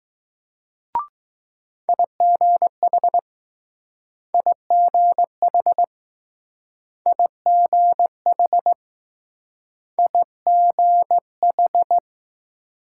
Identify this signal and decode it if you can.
Morse code, heard as a keyed tone, about 20 wpm, 700 Hz. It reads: IGH IGH IGH IGH